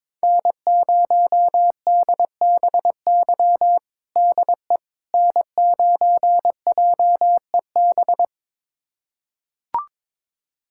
Morse code, heard as a keyed tone, about 22 wpm, 700 Hz. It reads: N0DBY DE N9JEB
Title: USA Callsign DE Callsign 22wpm